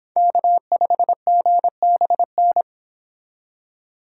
Morse code, heard as a keyed tone, about 25 wpm, 700 Hz. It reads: K5GBN